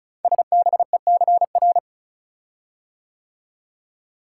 Morse code, heard as a keyed tone, about 36 wpm, 690 Hz. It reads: SBECR
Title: Code Groups - 5 Letters - 35wpm